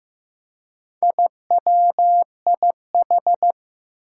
Morse code, heard as a keyed tone, about 16 wpm, 700 Hz. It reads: IWIH
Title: Four Challenging Letter-Number Combinations 15wpm